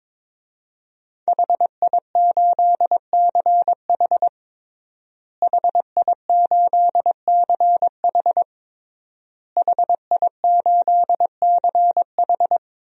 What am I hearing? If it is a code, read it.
Morse code, heard as a keyed tone, about 22 wpm, 700 Hz. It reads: HI8CH HI8CH HI8CH
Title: DXCC-100 Call Signs - 5 Characters - Repeated 3x - 22wpm